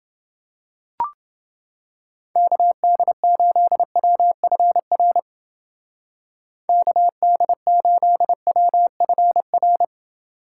Morse code, heard as a keyed tone, about 29 wpm, 700 Hz. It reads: KD8WFR KD8WFR